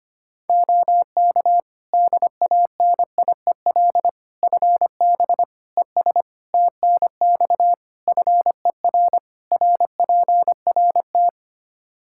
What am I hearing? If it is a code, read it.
Morse code, heard as a keyed tone, about 25 wpm, 700 Hz. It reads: OK DANIEL FB ES TNX FER RPRT
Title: K1USN SST QSO Practice 25wpm